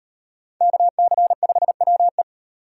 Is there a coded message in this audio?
Yes: Morse code from 0.6 to 2.2 s, heard as a keyed tone, about 36 wpm, 700 Hz. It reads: KC5WE